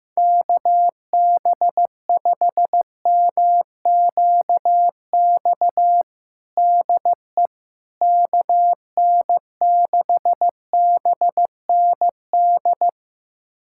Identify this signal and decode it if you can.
Morse code, heard as a keyed tone, about 15 wpm, 700 Hz. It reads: KB5MQX DE KN6BND
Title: USA Callsign DE Callsign 15wpm